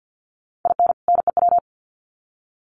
Morse code, heard as a keyed tone, about 50 wpm, 700 Hz. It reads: IN NEW
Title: Sets of 2 Words 50wpm